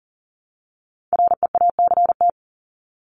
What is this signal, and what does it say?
Morse code, heard as a keyed tone, about 40 wpm, 700 Hz. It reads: REACT
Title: Top 1251-1500 Words 40wpm